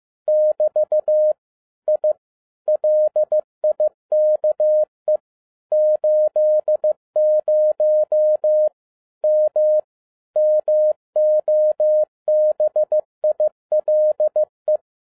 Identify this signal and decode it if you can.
Morse code, heard as a keyed tone, about 15 wpm, 610 Hz. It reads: <BT> I LIKE 80 M MOBILE
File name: 2012-01-mp3 / n07-ARA JR0DIL-15wpm